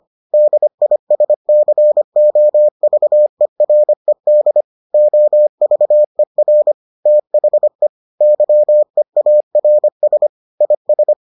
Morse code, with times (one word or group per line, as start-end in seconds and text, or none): 0.3-4.6 DISCOVERED
4.9-6.7 OVER
7.1-7.9 THE
8.2-10.3 YEARS
10.6-11.1 IS